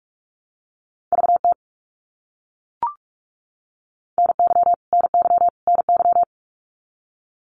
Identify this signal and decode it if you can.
Morse code, heard as a keyed tone, about 44 wpm, 700 Hz. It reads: VT NY NY NY
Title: US States Repeated 3x 45wpm